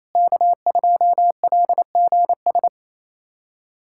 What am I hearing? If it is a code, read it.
Morse code, heard as a keyed tone, about 28 wpm, 700 Hz. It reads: K2LGS
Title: USA Callsign DE Callsign - 5 Characters - 28wpm